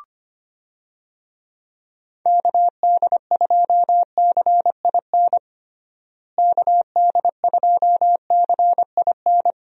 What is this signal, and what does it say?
Morse code, heard as a keyed tone, about 25 wpm, 700 Hz. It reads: KD2CIN KD2CIN
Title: USA Call Signs - 6 Characters - Repeated 3x - 25wpm